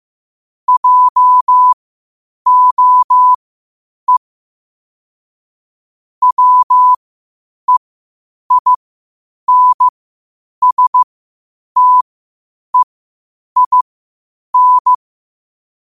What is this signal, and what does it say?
Morse code, heard as a keyed tone, about 15 wpm, spaced as for 8 wpm, 1000 Hz. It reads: JOE WEINSTEIN
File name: qotdfiles / 8wpm